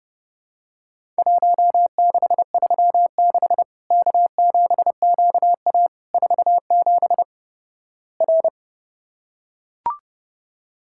Morse code, heard as a keyed tone, about 29 wpm, 700 Hz. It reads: 1636 K7QA 47 R
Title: Worked All Europe DX Contest - QTC Exchange - 30wpm